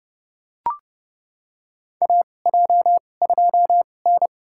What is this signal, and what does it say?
Morse code, heard as a keyed tone, about 30 wpm, spaced as for 22 wpm, 700 Hz. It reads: AJ2N
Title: USA Call Signs - 2x Character Spacing - 30wpm